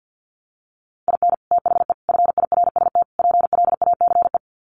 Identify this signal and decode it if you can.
Morse code, heard as a keyed tone, about 50 wpm, 700 Hz. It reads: IN THE FIRST PLACE
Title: Sets of 4 Words 50wpm